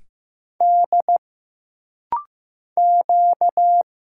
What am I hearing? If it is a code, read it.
Morse code, heard as a keyed tone, about 15 wpm, 700 Hz. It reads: D Q
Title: Single Letter-Number Alt - Rapid-Fire 15wpm